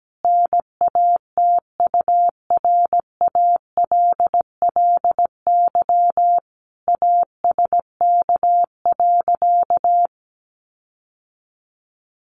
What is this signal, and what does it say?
Morse code, heard as a keyed tone, about 17 wpm, 700 Hz. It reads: NATURALLY ASK.
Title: Sentences from Top 1300 Words 17wpm